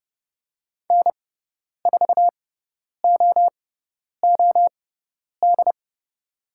Morse code, heard as a keyed tone, about 30 wpm, spaced as for 10 wpm, 700 Hz. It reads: N4OOD
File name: Lesson 17g - Letter u - Call Signs 30wpm